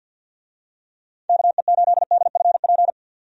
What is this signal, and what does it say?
Morse code, heard as a keyed tone, about 51 wpm, 700 Hz. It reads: KE8DFP
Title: USA Call Signs - 6 Characters - 50wpm